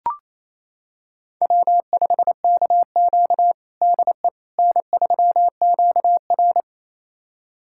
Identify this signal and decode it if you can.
Morse code, heard as a keyed tone, about 28 wpm, 700 Hz. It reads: W5KQ DE N3QR
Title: USA Callsign DE Callsign - 4 Characters - 28wpm